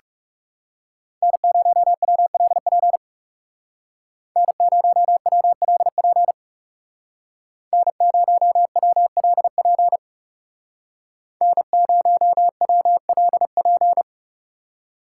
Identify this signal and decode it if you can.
Morse code, heard as a keyed tone, about 36 wpm, 700 Hz. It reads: N0WLP N0WLP N0WLP N0WLP
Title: Speed Racing - USA 1x3 Call Signs 30wpm